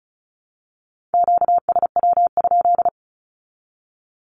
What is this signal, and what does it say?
Morse code, heard as a keyed tone, about 36 wpm, 700 Hz. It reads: QSW?
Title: Q-Codes with Spoken Meaning - 35wpm